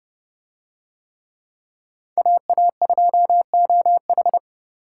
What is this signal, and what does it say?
Morse code, heard as a keyed tone, about 31 wpm, 700 Hz. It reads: AA2OH